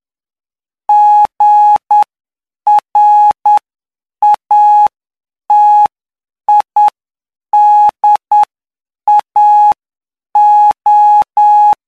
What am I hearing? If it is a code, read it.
Morse code, heard as a keyed tone, about 9 wpm, spaced as for 7 wpm, 840 Hz. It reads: GRATIDAO
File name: cw / py2lcd / 08-PY2LCD-Li��o 17